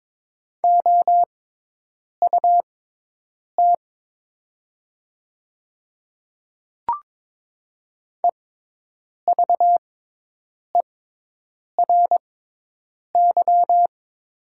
Morse code, heard as a keyed tone, about 22 wpm, spaced as for 8 wpm, 700 Hz. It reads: OUT EVERY
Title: 6x Character Spacing - Top 200 Words - 22wpm